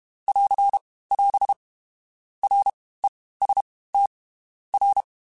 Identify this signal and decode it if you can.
Morse code, heard as a keyed tone, about 32 wpm, 800 Hz. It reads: <AR> <AS> R E S T R
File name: Groups@30 / Groups@30 mp3s / LFCYQ@30